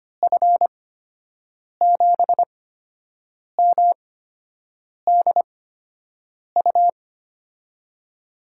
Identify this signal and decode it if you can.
Morse code, heard as a keyed tone, about 25 wpm, spaced as for 7 wpm, 700 Hz. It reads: F7MDU
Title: USA Call Signs - 6 Characters - 8x Character Spacing - 25wpm